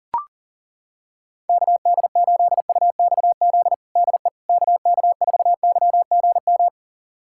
Morse code, heard as a keyed tone, about 40 wpm, 700 Hz. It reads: KD8UXZ DE KK4YGM
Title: USA Callsign DE Callsign - 6 Characters - 40wpm